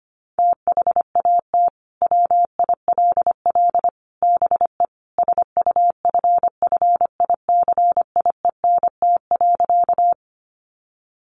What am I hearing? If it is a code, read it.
Morse code, heard as a keyed tone, about 25 wpm, 700 Hz. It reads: THAT WILL BE SUFFICIENT.
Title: Sentences from Top 1400 Words 25wpm